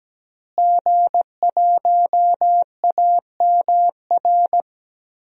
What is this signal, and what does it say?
Morse code, heard as a keyed tone, about 17 wpm, 700 Hz. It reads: G1AMR
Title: DXCC-100 Call Signs - 5 Characters - Repeated 3x - 17wpm